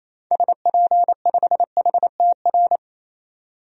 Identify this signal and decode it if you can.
Morse code, heard as a keyed tone, about 28 wpm, 700 Hz. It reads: SP5HTR
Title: DXCC-100 Call Signs - 6 Characters - 28wpm